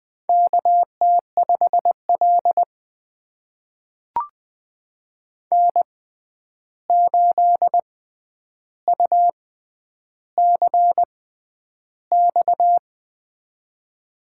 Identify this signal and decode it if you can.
Morse code, heard as a keyed tone, about 20 wpm, spaced as for 10 wpm, 700 Hz. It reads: KT5L N8UCX
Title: USA Call Signs - 6x Character Spacing - 20wpm